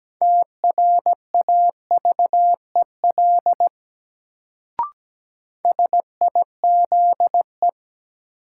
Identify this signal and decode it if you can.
Morse code, heard as a keyed tone, about 17 wpm, 700 Hz. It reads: TRAVEL SIZE
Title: Sentences from Top 400 Words - New Word Drill - 17wpm